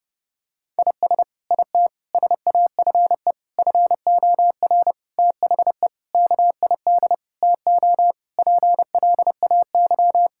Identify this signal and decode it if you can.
Morse code, heard as a keyed tone, about 32 wpm, 700 Hz. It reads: IS IT SAFE FOR THE KID TO PLAY
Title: Sentences from Top 1600 Words 30wpm